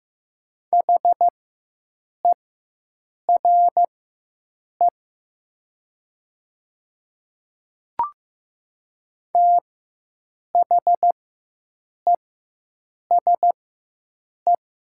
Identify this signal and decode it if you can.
Morse code, heard as a keyed tone, about 15 wpm, spaced as for 7 wpm, 700 Hz. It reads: HERE THESE
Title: 4x Character Spacing - Top 200 Words - 15wpm